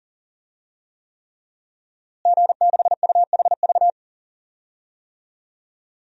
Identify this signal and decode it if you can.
Morse code, heard as a keyed tone, about 40 wpm, 700 Hz. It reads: G6UHV
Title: DXCC-100 Call Signs - 5 Characters - Repeated 3x - 40wpm